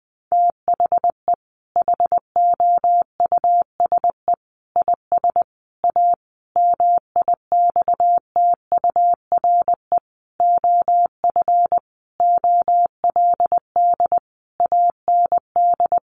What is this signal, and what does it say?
Morse code, heard as a keyed tone, about 20 wpm, 700 Hz. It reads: THE HOUSE IS A MIXTURE OF OLD AND
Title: Sentences from Top 1500 Words 20wpm